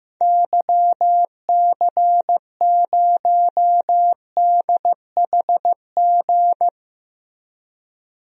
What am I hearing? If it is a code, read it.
Morse code, heard as a keyed tone, about 15 wpm, 700 Hz. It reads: YC0DHG
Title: DXCC-100 Call Signs - 6 Characters - 15wpm